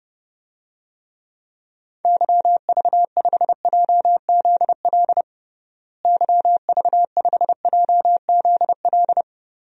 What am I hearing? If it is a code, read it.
Morse code, heard as a keyed tone, about 32 wpm, 700 Hz. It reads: YV5JZL YV5JZL
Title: DXCC-100 Call Signs - 6 Characters - Repeated 2x - 30wpm